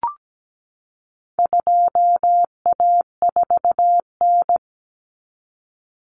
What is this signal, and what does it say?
Morse code, heard as a keyed tone, about 17 wpm, 700 Hz. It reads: 2A4N
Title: Common Cut Numbers - 4-Digit 17wpm